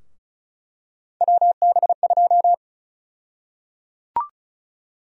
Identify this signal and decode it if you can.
Morse code, heard as a keyed tone, about 36 wpm, 700 Hz. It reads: WB2
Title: USA Call Sign Prefixes - 35wpm